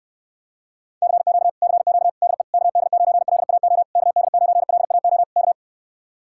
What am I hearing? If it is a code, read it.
Morse code, heard as a keyed tone, about 67 wpm, 700 Hz. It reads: CQ CQ DE KD9LAO KD9LAO K